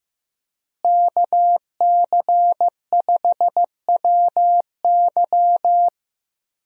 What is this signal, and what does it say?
Morse code, heard as a keyed tone, about 16 wpm, 700 Hz. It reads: KC5WY
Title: USA Call Signs - 5 Characters - Repeated 3x - 15wpm